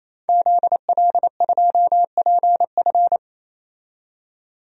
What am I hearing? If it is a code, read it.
Morse code, heard as a keyed tone, about 28 wpm, 700 Hz. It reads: ZL2PF
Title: DXCC-100 Call Signs - 5 Characters - 28wpm